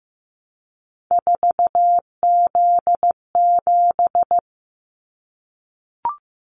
Morse code, heard as a keyed tone, about 15 wpm, 700 Hz. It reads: 4Z7
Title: Callsign Prefixes for Top 50 DXCC - 15wpm